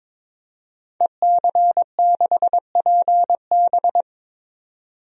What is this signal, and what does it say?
Morse code, heard as a keyed tone, about 22 wpm, 700 Hz. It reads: EC6PB
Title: DXCC-100 Call Signs - 5 Characters - 22wpm